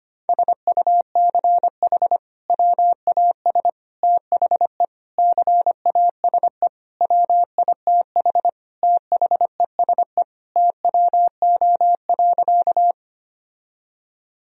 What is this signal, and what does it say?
Morse code, heard as a keyed tone, about 25 wpm, 700 Hz. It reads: SUCH WAS THE CASE WITH THESE TWO.
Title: Sentences from Top 1000 Words - Review Part 1 - 25wpm